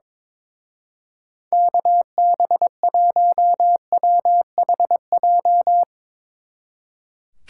Morse code, heard as a keyed tone, about 22 wpm, 700 Hz. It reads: KB1WHJ
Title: Speed Racing - USA 2x3 Call Signs 22wpm